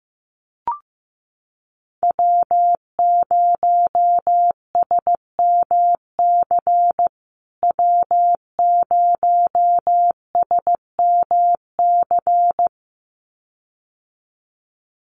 Morse code, heard as a keyed tone, about 15 wpm, 700 Hz. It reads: W0SMC W0SMC
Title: USA Call Signs Repeated 2x 15wpm